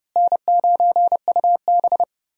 Morse code, heard as a keyed tone, about 30 wpm, 700 Hz. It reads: N9UB
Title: USA Call Signs - 4 Characters - Repeated 3x - 30wpm